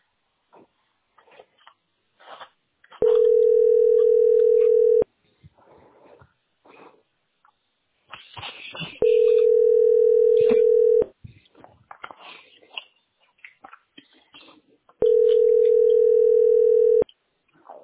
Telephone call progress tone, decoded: ringback tone

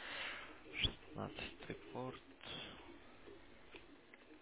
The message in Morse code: none

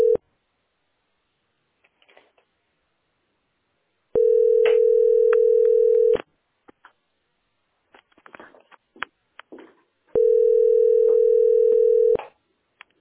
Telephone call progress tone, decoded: ringback tone

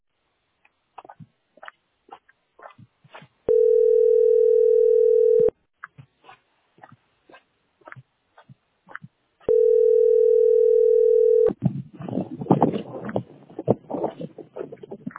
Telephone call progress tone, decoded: ringback tone

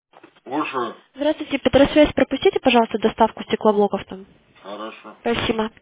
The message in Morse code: none

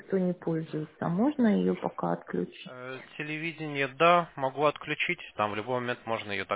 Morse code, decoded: none

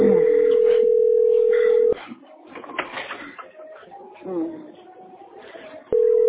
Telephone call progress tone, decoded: ringback tone

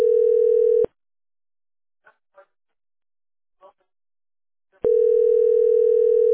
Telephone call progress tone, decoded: ringback tone

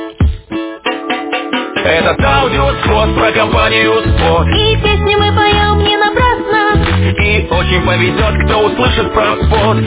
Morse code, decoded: none

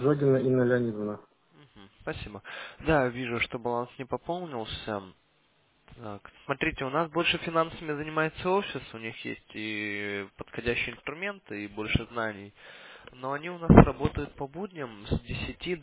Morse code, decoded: none

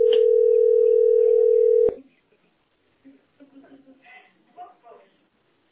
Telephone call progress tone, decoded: ringback tone

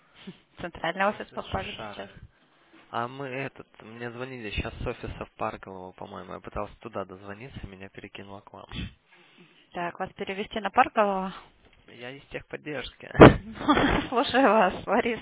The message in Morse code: none

